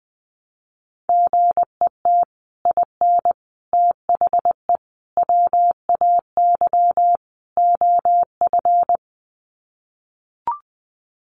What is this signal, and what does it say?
Morse code, heard as a keyed tone, about 19 wpm, 700 Hz. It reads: GET IN THE WAY OF
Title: Sets of 5 Words Encore 20wpm